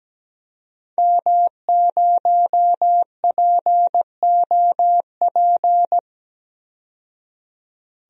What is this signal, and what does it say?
Morse code, heard as a keyed tone, about 17 wpm, 700 Hz. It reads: M0POP
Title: DXCC-100 Call Signs - 5 Characters - 17wpm